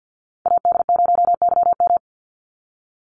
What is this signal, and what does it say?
Morse code, heard as a keyed tone, about 50 wpm, 700 Hz. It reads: AD0YM
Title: NA SOTA Chaser Callsigns 50wpm